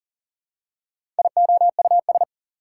Morse code, heard as a keyed tone, about 40 wpm, 700 Hz. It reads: IOUS
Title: Prefixes and Suffixes 40wpm